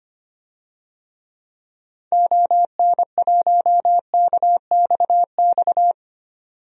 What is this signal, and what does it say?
Morse code, heard as a keyed tone, about 25 wpm, 700 Hz. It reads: ON1KXX
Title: DXCC-100 Call Signs - 6 Characters - 25wpm